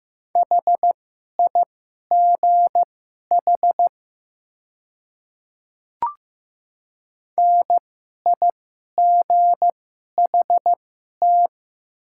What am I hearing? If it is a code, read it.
Morse code, heard as a keyed tone, about 15 wpm, spaced as for 11 wpm, 700 Hz. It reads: HIGH NIGHT
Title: 2x Character Spacing - Top 300 Words - 15wpm